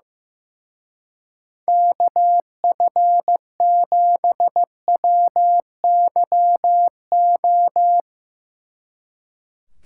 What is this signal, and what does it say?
Morse code, heard as a keyed tone, about 17 wpm, 700 Hz. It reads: KF7WYO
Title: Speed Racing - USA 2x3 Call Signs 15wpm